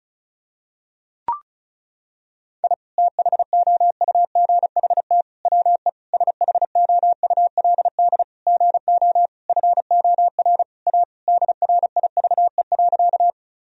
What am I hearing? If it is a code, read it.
Morse code, heard as a keyed tone, about 36 wpm, 700 Hz. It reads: I THOUGHT WE SHOULD GO FOR A DRIVE.
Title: Sentences from Top 1000 Words - Review Part 2 - 35wpm